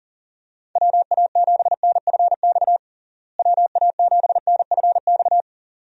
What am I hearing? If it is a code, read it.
Morse code, heard as a keyed tone, about 40 wpm, 700 Hz. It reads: WA7NFX WA7NFX
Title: USA Call Signs - 6 Characters - Repeated 2x - 40wpm